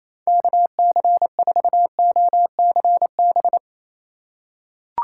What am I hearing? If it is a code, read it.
Morse code, heard as a keyed tone, about 28 wpm, 700 Hz. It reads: KC4OCB
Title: USA Call Signs - 6 Characters - 3x Character Spacing - 28wpm